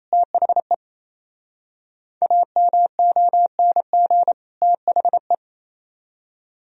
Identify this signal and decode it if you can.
Morse code, heard as a keyed tone, about 36 wpm, 700 Hz. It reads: THE AMONG THE